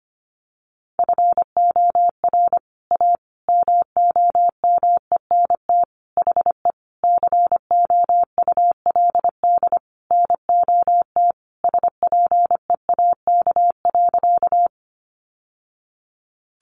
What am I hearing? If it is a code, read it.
Morse code, heard as a keyed tone, about 25 wpm, 700 Hz. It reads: FOR A MOMENT HE COULD NOT SPEAK.
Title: Sentences from Top 1100 Words 25wpm